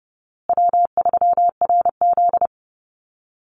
Morse code, heard as a keyed tone, about 30 wpm, 700 Hz. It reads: W3RZ
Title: Easy Call Signs Repeated 3x - 30wpm